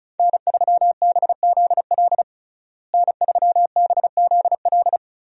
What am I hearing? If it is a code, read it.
Morse code, heard as a keyed tone, about 36 wpm, 700 Hz. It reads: N3BZL N3BZL
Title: USA Call Signs - 5 Characters - Repeated 3x - 35wpm